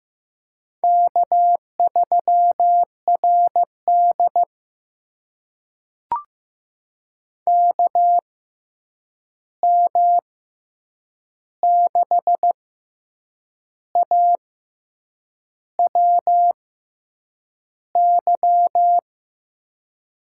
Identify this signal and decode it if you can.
Morse code, heard as a keyed tone, about 15 wpm, spaced as for 7 wpm, 700 Hz. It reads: K3RD KM6AWY
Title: USA Call Signs - 6x Character Spacing - 15wpm